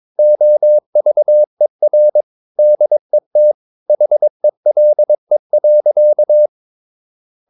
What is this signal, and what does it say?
Morse code, heard as a keyed tone, about 22 wpm, 600 Hz. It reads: OVER DET HELE.
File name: Ferie QRS 22wpm